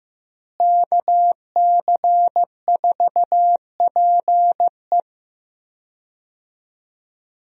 Morse code, heard as a keyed tone, about 15 wpm, 700 Hz. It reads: KC4PE
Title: USA Call Signs - 5 Characters - Repeated 2x - 15wpm